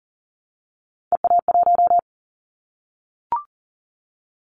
Morse code, heard as a keyed tone, about 40 wpm, 700 Hz. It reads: EA1
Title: Callsign Prefixes for Top 25 DXCC - 40wpm